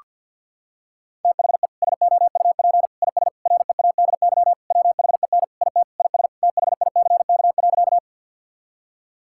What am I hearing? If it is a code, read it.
Morse code, heard as a keyed tone, about 50 wpm, 700 Hz. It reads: THE SOUP IS READY WHEN IT IS THICK.
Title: Sentences from Top 1600 Words 50wpm